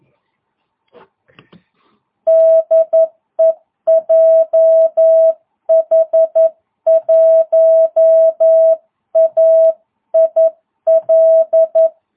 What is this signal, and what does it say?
Morse code, heard as a keyed tone, about 11 wpm, 650 Hz. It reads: DE JH1AIL